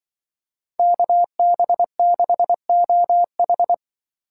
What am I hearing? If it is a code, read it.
Morse code, heard as a keyed tone, about 23 wpm, 700 Hz. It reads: KB6OH
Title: Speed Racing - USA 2x2 Call Signs 20wpm